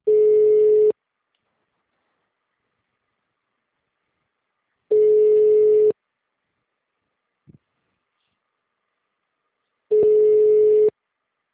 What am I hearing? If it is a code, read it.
Ringback tone (Europe, 425 Hz).